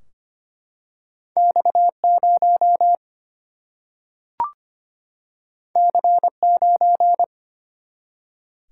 Morse code, heard as a keyed tone, about 25 wpm, 700 Hz. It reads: X0 C9
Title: Single-Letter Single-Number 25wpm